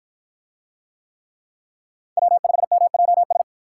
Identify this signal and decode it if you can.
Morse code, heard as a keyed tone, about 46 wpm, 700 Hz. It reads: W5MJS